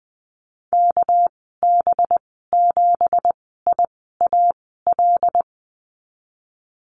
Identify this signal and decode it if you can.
Morse code, heard as a keyed tone, about 20 wpm, spaced as for 15 wpm, 700 Hz. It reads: KB7IAL
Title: 200 Lesson 32g - Letter k - Call Signs 20wpm